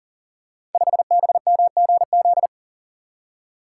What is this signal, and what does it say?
Morse code, heard as a keyed tone, about 42 wpm, 700 Hz. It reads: 5BMGZ
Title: Code Groups - 5 Letter:Numbers - 40wpm